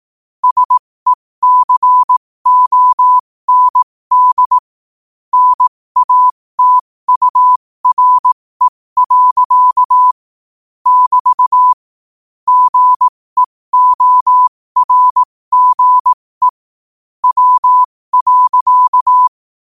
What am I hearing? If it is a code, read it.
Morse code, heard as a keyed tone, about 18 wpm, spaced as for 15 wpm, 1000 Hz. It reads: SECOND NATURE. <BT> GEORGE W.
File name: qotdfiles / 15wpm